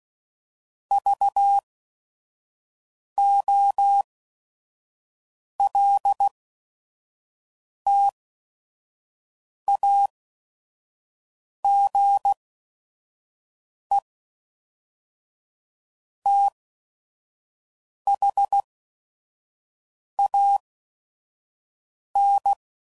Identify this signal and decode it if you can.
Morse code, heard as a keyed tone, about 16 wpm, spaced as for 5 wpm, 800 Hz. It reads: VOLTAGE THAN
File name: CW Code Groups @5WPM